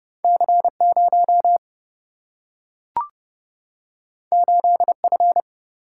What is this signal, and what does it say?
Morse code, heard as a keyed tone, about 30 wpm, 700 Hz. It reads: C0 8F
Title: Single-Letter Single-Number 30wpm